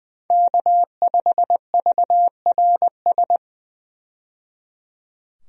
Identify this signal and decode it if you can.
Morse code, heard as a keyed tone, about 20 wpm, 700 Hz. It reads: K5VRS
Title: USA Call Signs - 5 Characters - 20wpm